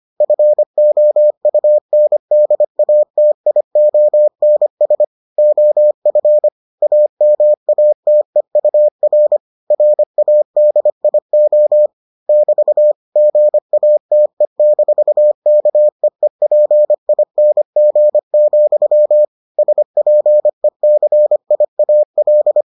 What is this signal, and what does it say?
Morse code, heard as a keyed tone, about 25 wpm, 600 Hz. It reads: FOUNDATIONS OF AMATEUR RADIO <BT> GATE-KEEPING, SPECIAL